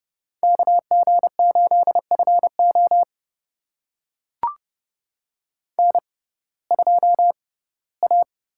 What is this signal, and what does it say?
Morse code, heard as a keyed tone, about 30 wpm, spaced as for 18 wpm, 700 Hz. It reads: KG8FO N2A
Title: USA Call Signs - 5 Characters - 6x Character Spacing - 30wpm